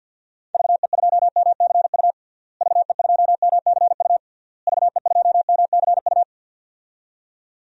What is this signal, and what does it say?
Morse code, heard as a keyed tone, about 48 wpm, 700 Hz. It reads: VE2MCU VE2MCU VE2MCU
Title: DXCC-100 Call Signs - 6 Characters - Repeated 3x - 50wpm